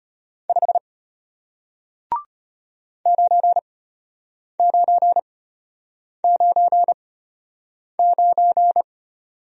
Morse code, heard as a keyed tone, about 33 wpm, spaced as for 8 wpm, 700 Hz. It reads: 5 9999